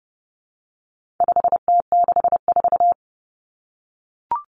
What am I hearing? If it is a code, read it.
Morse code, heard as a keyed tone, about 30 wpm, 700 Hz. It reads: 5T64
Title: Common Cut Numbers - 4-Digit 30wpm